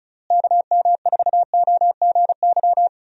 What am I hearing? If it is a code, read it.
Morse code, heard as a keyed tone, about 36 wpm, 700 Hz. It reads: KM4OGY
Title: USA Call Signs - 6 Characters - Repeated 3x - 35wpm